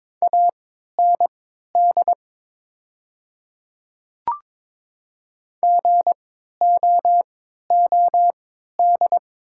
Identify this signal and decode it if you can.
Morse code, heard as a keyed tone, about 22 wpm, spaced as for 13 wpm, 700 Hz. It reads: AND GOOD
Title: 3x Character Spacing - Top 300 Words - 22wpm